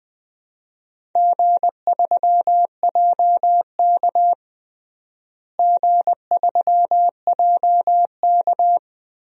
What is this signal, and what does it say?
Morse code, heard as a keyed tone, about 20 wpm, 700 Hz. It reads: G3JK G3JK
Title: DXCC-100 Call Signs - 4 Characters - Repeated 2x - 20wpm